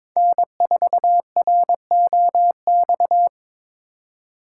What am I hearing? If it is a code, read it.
Morse code, heard as a keyed tone, about 26 wpm, 700 Hz. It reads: N4ROX
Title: Speed Racing - USA 1x3 Call Signs 22wpm